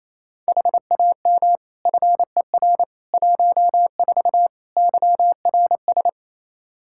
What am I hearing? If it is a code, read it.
Morse code, heard as a keyed tone, about 27 wpm, 700 Hz. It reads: HAM FER 14 YRS
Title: Practice QSO - Exchange 3 - More Info - 28wpm